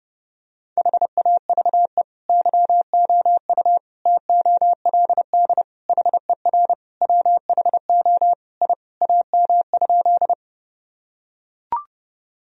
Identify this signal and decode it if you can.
Morse code, heard as a keyed tone, about 31 wpm, 700 Hz. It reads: HAVE YOU TOLD HER WHO I AM?
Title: Sentences from Top 1000 Words - Review Part 2 - 30wpm